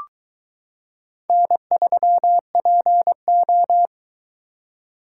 Morse code, heard as a keyed tone, about 22 wpm, 700 Hz. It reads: N3PO